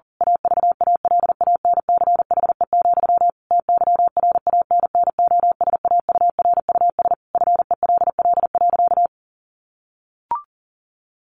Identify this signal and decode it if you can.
Morse code, heard as a keyed tone, about 37 wpm, 700 Hz. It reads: AVALANCHE, TYRANNOSAURUS FELL.
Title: A Sound Of Thunder 40wpm